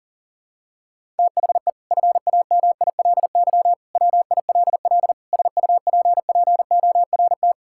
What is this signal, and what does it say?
Morse code, heard as a keyed tone, about 37 wpm, 700 Hz. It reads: THE FAMILY WILL SUPPORT
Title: Sentences from Top 1000 Words - Review Part 5 - 40wpm